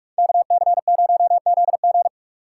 Morse code, heard as a keyed tone, about 45 wpm, 700 Hz. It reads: KC0ZG